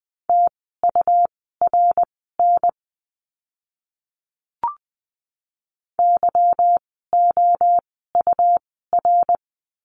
Morse code, heard as a keyed tone, about 20 wpm, spaced as for 15 wpm, 700 Hz. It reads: TURN YOUR
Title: 244 Lesson 39f - Letter z - Words 20wpm